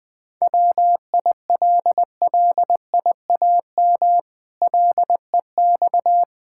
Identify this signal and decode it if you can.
Morse code, heard as a keyed tone, about 21 wpm, 700 Hz. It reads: WILLIAM LEX
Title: Pangram Challenge v2 - 20wpm